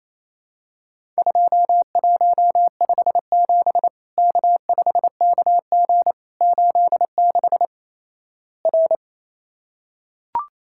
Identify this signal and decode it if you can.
Morse code, heard as a keyed tone, about 28 wpm, 700 Hz. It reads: 2157 K5KG 86 R
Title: Worked All Europe DX Contest - QTC Exchange - 28wpm